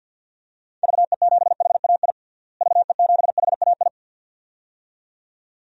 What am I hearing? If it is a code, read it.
Morse code, heard as a keyed tone, about 47 wpm, 700 Hz. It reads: VE7HAI VE7HAI
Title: DXCC-100 Call Signs - 6 Characters - Repeated 2x - 50wpm